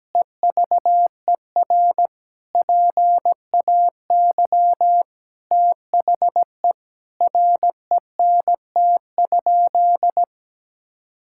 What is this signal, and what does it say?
Morse code, heard as a keyed tone, about 17 wpm, 700 Hz. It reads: EVER PAY THE RENT?
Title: Sentences from Top 1000 Words - Review Part 5 - 17wpm